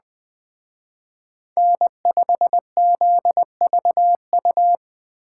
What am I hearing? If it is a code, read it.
Morse code, heard as a keyed tone, about 20 wpm, 700 Hz. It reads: N5ZVU
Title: Speed Racing - USA 1x3 Call Signs 17wpm